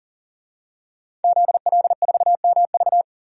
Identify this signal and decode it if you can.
Morse code, heard as a keyed tone, about 40 wpm, 700 Hz. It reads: ZL4MV
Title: DXCC-100 Call Signs - 5 Characters - 40wpm